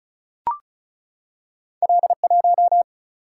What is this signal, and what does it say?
Morse code, heard as a keyed tone, about 36 wpm, 700 Hz. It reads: L1
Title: Single-Letter Single-Number 35wpm